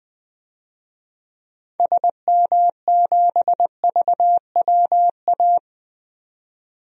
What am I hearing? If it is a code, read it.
Morse code, heard as a keyed tone, about 21 wpm, 700 Hz. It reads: SM7VWA
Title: DXCC-100 Call Signs - 6 Characters - 20wpm